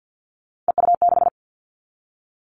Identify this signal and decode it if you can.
Morse code, heard as a keyed tone, about 50 wpm, 700 Hz. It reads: EU6
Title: Callsign Prefixes for Top 50 DXCC - 50wpm